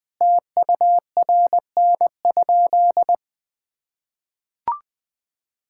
Morse code, heard as a keyed tone, about 21 wpm, 700 Hz. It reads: TURN?
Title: Sentences from Top 1600 Words 20wpm